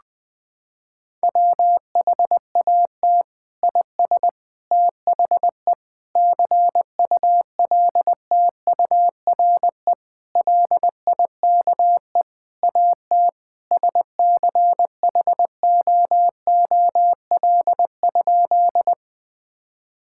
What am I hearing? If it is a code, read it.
Morse code, heard as a keyed tone, about 20 wpm, 700 Hz. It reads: WHAT IS THE CULTURE LIKE AT SCHOOL?
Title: Sentences from Top 1000 Words - Review Part 6 - 20wpm